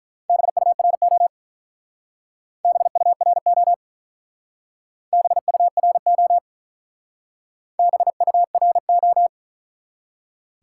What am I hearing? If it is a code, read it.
Morse code, heard as a keyed tone, about 44 wpm, 700 Hz. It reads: BURO BURO BURO BURO